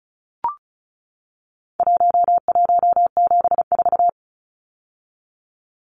Morse code, heard as a keyed tone, about 34 wpm, 700 Hz. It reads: 1174
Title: Numbers - Contest Style - 4-Digit 35wpm